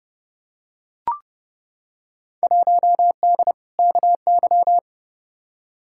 Morse code, heard as a keyed tone, about 30 wpm, 700 Hz. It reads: 1D KY
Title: Field Day Exchanges - 30wpm